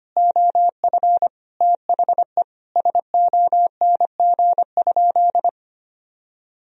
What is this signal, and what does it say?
Morse code, heard as a keyed tone, about 25 wpm, 700 Hz. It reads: OF THE SONG?
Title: Sentences from Top 1000 Words - Review Part 2 - 25wpm